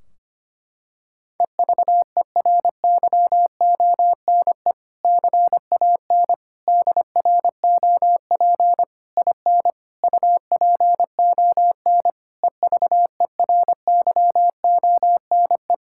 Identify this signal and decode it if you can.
Morse code, heard as a keyed tone, about 25 wpm, 700 Hz. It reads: EVERYONE CAN DROP IN UPON EVERYONE
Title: Sentences from Top 1600 Words 25wpm